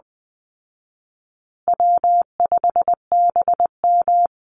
Morse code, heard as a keyed tone, about 20 wpm, 700 Hz. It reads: W5BM
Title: Easy Call Signs Repeated 2x - 20wpm